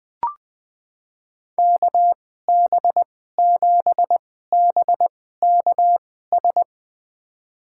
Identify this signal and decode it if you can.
Morse code, heard as a keyed tone, about 20 wpm, spaced as for 14 wpm, 700 Hz. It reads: KB7BKS
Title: USA Call Signs - 6 Characters - 2x Character Spacing - 20wpm